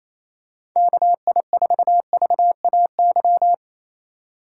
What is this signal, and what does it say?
Morse code, heard as a keyed tone, about 28 wpm, 700 Hz. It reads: KI4VAY
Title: USA Call Signs - 3x Character Spacing - 28wpm